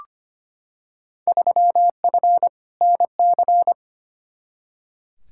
Morse code, heard as a keyed tone, about 25 wpm, 700 Hz. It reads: 3F NC
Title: Field Day Exchanges Encore - 25wpm